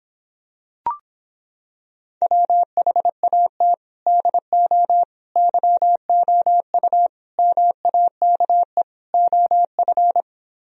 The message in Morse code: WHAT DO YOU MAKE OF

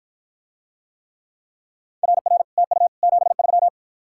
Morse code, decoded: RR TU 73

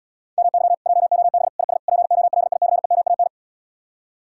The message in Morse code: MY JOB IS POLICEMAN

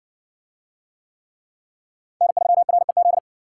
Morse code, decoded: N3REZ

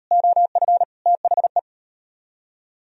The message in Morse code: OF THE

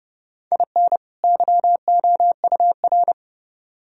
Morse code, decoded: IN YOUR

